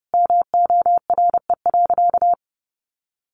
Morse code, heard as a keyed tone, about 30 wpm, 700 Hz. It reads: MORE.
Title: Sentences from Top 500 Words 30wpm